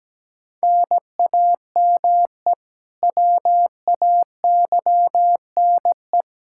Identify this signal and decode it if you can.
Morse code, heard as a keyed tone, about 17 wpm, 700 Hz. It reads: NAME WAYNE
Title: Practice QSO - Exchange 1 - Beginning - 17wpm